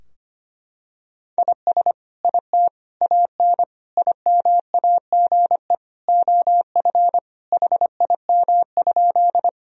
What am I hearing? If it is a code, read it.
Morse code, heard as a keyed tone, about 25 wpm, 700 Hz. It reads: IS IT AN IMAGE OF HIM?
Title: Sentences from Top 1000 Words - Review Part 4 - 25wpm